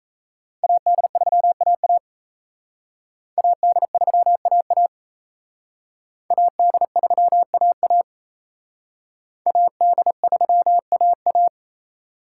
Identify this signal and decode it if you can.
Morse code, heard as a keyed tone, about 38 wpm, 700 Hz. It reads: AD3AA AD3AA AD3AA AD3AA